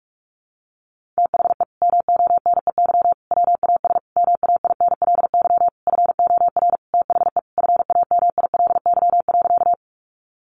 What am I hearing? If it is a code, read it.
Morse code, heard as a keyed tone, about 44 wpm, 700 Hz. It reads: THE MONEY WAS MAINLY FOR THE FAMILY.
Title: Sentences from Top 1400 Words 45wpm